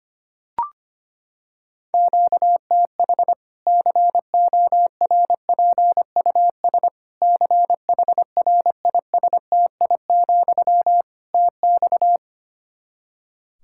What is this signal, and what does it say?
Morse code, heard as a keyed tone, about 25 wpm, 700 Hz. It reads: QTH CORPUS CHRISTI, TX
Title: K1USN SST QSO Practice 25wpm